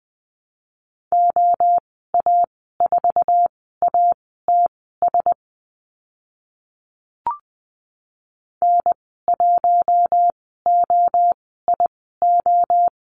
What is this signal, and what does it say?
Morse code, heard as a keyed tone, about 20 wpm, spaced as for 14 wpm, 700 Hz. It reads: OA4ATS N1OIO
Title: 082 Lesson 14g - Number 2 - Call Signs 20wpm